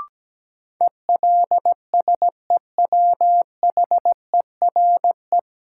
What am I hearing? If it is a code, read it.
Morse code, heard as a keyed tone, about 17 wpm, 700 Hz. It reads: ELSEWHERE
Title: Sentences from Top 1600 Words - New Word Drill Alt - 17wpm